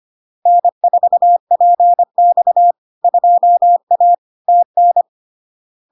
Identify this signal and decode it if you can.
Morse code, heard as a keyed tone, about 25 wpm, 700 Hz. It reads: N4PX 2A TN
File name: Log-in-CW-25wpm-700-Hz